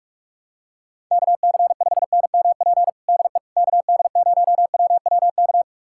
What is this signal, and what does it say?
Morse code, heard as a keyed tone, about 42 wpm, 700 Hz. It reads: KC5NMP DE KD0WWK
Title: USA Callsign DE Callsign 45wpm